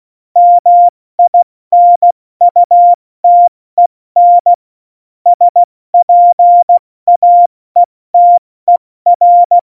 Morse code, heard as a keyed tone, about 16 wpm, 700 Hz. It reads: MINUTEN SPAETER
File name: Morsekurs / Witze / Witz 02. Oktober 2018, 70 F80